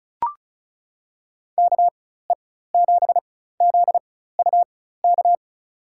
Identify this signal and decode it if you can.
Morse code, heard as a keyed tone, about 36 wpm, spaced as for 16 wpm, 700 Hz. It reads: KE7ZUK
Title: USA Call Signs - 6 Characters - 4x Character Spacing - 35wpm